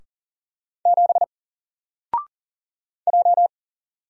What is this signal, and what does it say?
Morse code, heard as a keyed tone, about 40 wpm, 700 Hz. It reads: 7 J
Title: Single Character (Letter:Number:Punctuation:Prosign) Alt 40wpm